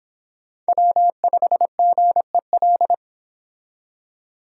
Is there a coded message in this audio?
Yes: Morse code from 0.7 to 2.9 s, heard as a keyed tone, about 25 wpm, 700 Hz. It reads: W5GEL